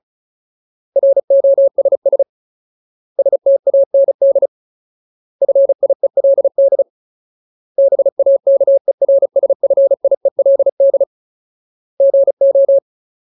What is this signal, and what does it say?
Morse code, heard as a keyed tone, about 34 wpm, 550 Hz. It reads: ROSS STAND FIELD BAKERSFIELD GO
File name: Ultimate Word List / 35WPM Ultimate